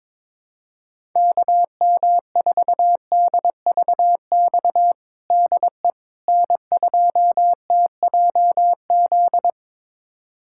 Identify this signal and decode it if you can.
Morse code, heard as a keyed tone, about 22 wpm, 700 Hz. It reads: KM4DVX DE N2TJZ
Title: USA Callsign DE Callsign 22wpm